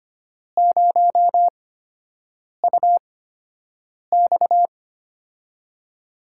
Morse code, heard as a keyed tone, about 25 wpm, spaced as for 7 wpm, 700 Hz. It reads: 0UX
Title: USA Call Signs - 4 Characters - 8x Character Spacing - 25wpm